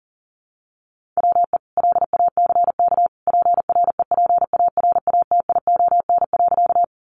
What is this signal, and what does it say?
Morse code, heard as a keyed tone, about 40 wpm, 700 Hz. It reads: WE LACK PREPARATION.
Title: Sentences from Top 1400 Words 40wpm